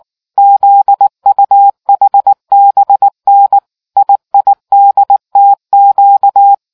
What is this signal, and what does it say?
Morse code, heard as a keyed tone, about 19 wpm, 800 Hz. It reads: ZUHBN IIDTQ